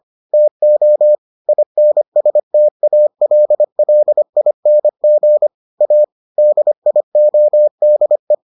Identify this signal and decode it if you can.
Morse code, heard as a keyed tone, about 25 wpm, 600 Hz. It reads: TO INSTALLING A DIODE